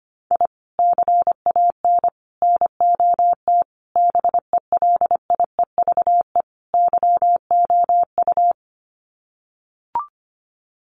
Morse code, heard as a keyed tone, about 25 wpm, 700 Hz. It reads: I CAN NOT BELIEVE YOU
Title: Sets of 5 Words Encore 25wpm